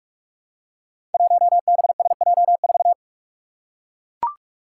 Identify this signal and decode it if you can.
Morse code, heard as a keyed tone, about 44 wpm, 700 Hz. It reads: 1BSJ4
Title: Five Challenging Letter-Number Combinations 45wpm